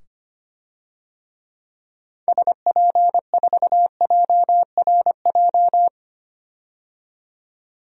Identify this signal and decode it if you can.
Morse code, heard as a keyed tone, about 25 wpm, 700 Hz. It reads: SP4JRJ
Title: DXCC-100 Call Signs - 6 Characters - 25wpm